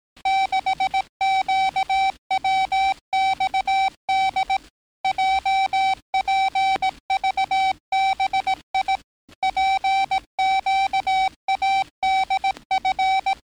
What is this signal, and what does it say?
Morse code, heard as a keyed tone, about 18 wpm, 770 Hz. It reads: 6QWXD JPVBI PQADF
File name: training / NAVPERS 11049to11060 / NAVPERS 11058 RA RB